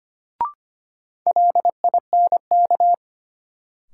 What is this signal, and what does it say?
Morse code, heard as a keyed tone, about 25 wpm, 700 Hz. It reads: LINK
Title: Sentences from Top 1000 Words - Review Part 7 - New Word Drill - 25wpm